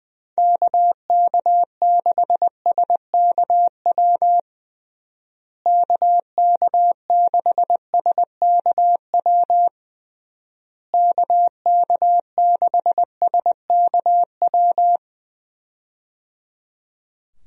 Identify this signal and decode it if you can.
Morse code, heard as a keyed tone, about 20 wpm, 700 Hz. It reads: KK6SKW KK6SKW KK6SKW
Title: USA Call Signs - 6 Characters - Repeated 3x - 20wpm